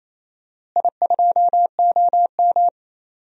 Morse code, heard as a keyed tone, about 28 wpm, 700 Hz. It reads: I2OM
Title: DXCC-100 Call Signs - 4 Characters - Repeated 3x - 28wpm